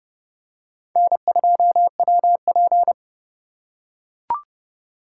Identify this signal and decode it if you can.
Morse code, heard as a keyed tone, about 30 wpm, 700 Hz. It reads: N2WP
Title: USA Call Signs - 4 Characters - 4x Character Spacing - 30wpm